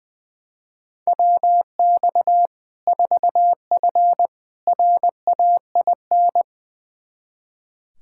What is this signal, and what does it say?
Morse code, heard as a keyed tone, about 20 wpm, 700 Hz. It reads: WX 4F RAIN
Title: Practice QSO - Exchange 2 - Info - 20wpm